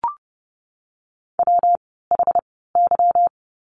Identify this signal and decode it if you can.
Morse code, heard as a keyed tone, about 30 wpm, spaced as for 17 wpm, 700 Hz. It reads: WHY
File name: course / 30:10wpm / 230 Lesson 37f - Letter x - Words 30wpm